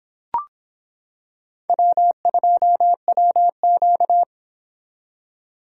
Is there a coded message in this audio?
Yes: Morse code from 1.7 to 4.2 s, heard as a keyed tone, about 24 wpm, 700 Hz. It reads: W2WQ